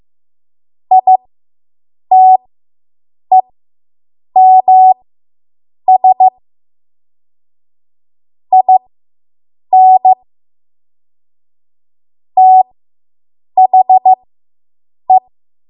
Morse code, heard as a keyed tone, about 15 wpm, spaced as for 7 wpm, 750 Hz. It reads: ITEMS IN THE